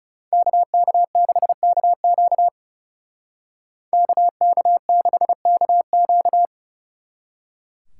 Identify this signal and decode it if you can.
Morse code, heard as a keyed tone, about 39 wpm, 700 Hz. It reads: KK6KQ KK6KQ